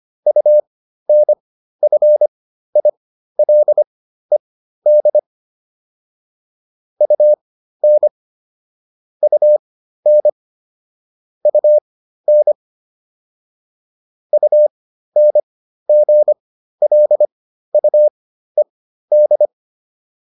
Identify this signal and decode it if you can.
Morse code, heard as a keyed tone, about 25 wpm, spaced as for 13 wpm, 600 Hz. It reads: UNFILED UN UN UN UNGLUED